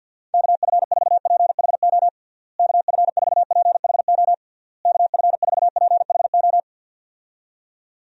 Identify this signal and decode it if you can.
Morse code, heard as a keyed tone, about 51 wpm, 700 Hz. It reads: KF4PHO KF4PHO KF4PHO
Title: USA Call Signs - 6 Characters - Repeated 3x - 50wpm